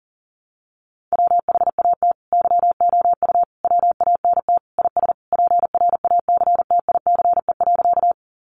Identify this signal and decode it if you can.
Morse code, heard as a keyed tone, about 40 wpm, 700 Hz. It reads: WHAT YOU WANT IS PRACTICE.
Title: Sentences from Top 700 Words 40wpm